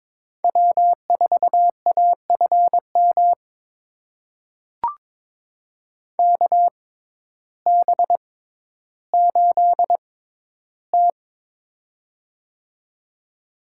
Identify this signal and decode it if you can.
Morse code, heard as a keyed tone, about 21 wpm, spaced as for 12 wpm, 700 Hz. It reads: W4AFM KB8T
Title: USA Call Signs - 6x Character Spacing - 22wpm